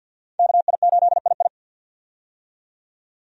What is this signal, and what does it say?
Morse code, heard as a keyed tone, about 51 wpm, 700 Hz. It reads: KI8II
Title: USA Call Signs - 5 Characters - 50wpm